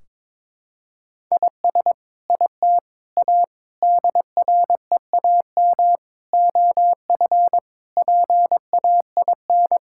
Morse code, heard as a keyed tone, about 22 wpm, 700 Hz. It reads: IS IT A DREAM OF PAIN
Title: Sentences from Top 1600 Words 22wpm